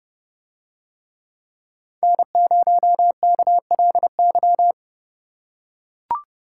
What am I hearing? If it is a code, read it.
Morse code, heard as a keyed tone, about 30 wpm, 700 Hz. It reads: N0KLY